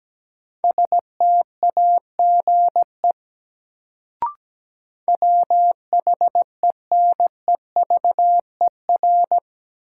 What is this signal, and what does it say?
Morse code, heard as a keyed tone, about 17 wpm, 700 Hz. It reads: STAGE WHENEVER
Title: Sentences from Top 1200 Words - New Word Drill - 17wpm